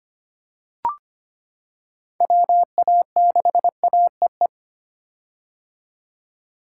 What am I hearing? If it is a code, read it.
Morse code, heard as a keyed tone, about 25 wpm, 700 Hz. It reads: WA6AEE
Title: K1USN SST QSO Practice - 25wpm